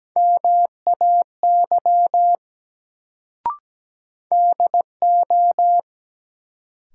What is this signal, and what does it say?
Morse code, heard as a keyed tone, about 17 wpm, 700 Hz. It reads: MAY DO
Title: Sentences from Top 100 Words - New Word Drill - 17wpm